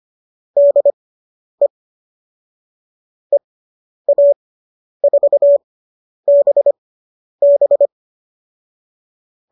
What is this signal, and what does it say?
Morse code, heard as a keyed tone, about 25 wpm, spaced as for 10 wpm, 590 Hz. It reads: DE EA4BB